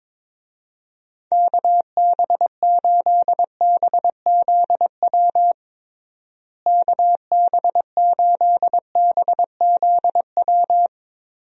Morse code, heard as a keyed tone, about 22 wpm, 700 Hz. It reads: KB8BZW KB8BZW
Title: USA Call Signs - 6 Characters - Repeated 3x - 22wpm